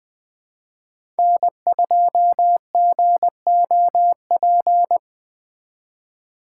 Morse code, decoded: N2GOP

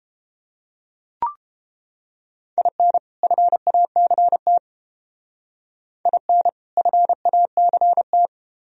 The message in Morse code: IN FACT IN FACT

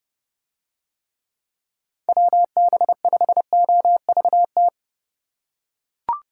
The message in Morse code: WB5OVT

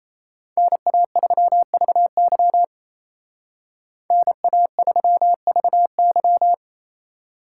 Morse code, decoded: NA3VY NA3VY

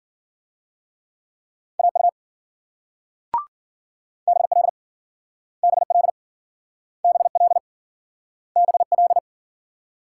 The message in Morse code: AU BL BL BL BL

